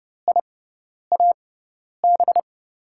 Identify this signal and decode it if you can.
Morse code, heard as a keyed tone, about 30 wpm, spaced as for 10 wpm, 700 Hz. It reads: IAB
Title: USA Call Signs - 6 Characters - 6x Character Spacing - 30wpm